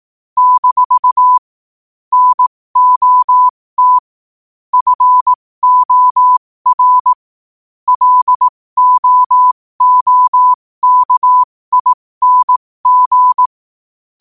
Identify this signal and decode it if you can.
Morse code, heard as a keyed tone, about 18 wpm, spaced as for 15 wpm, 1000 Hz. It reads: - NOT FOR LOOKING
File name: qotdfiles / 15wpm